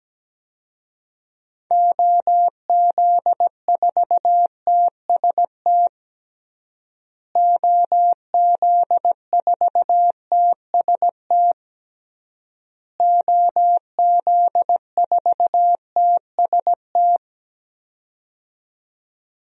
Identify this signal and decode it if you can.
Morse code, heard as a keyed tone, about 17 wpm, 700 Hz. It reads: OZ4TST OZ4TST OZ4TST
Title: DXCC-100 Call Signs - 6 Characters - Repeated 3x - 17wpm